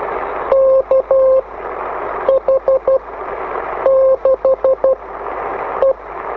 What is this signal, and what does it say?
Morse code, heard as a keyed tone, about 12 wpm, spaced as for 7 wpm, 530 Hz. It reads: KH6E